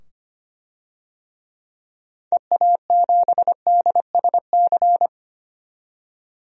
Morse code, heard as a keyed tone, about 25 wpm, 700 Hz. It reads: EA7DSC
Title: DXCC-100 Call Signs - 6 Characters - 25wpm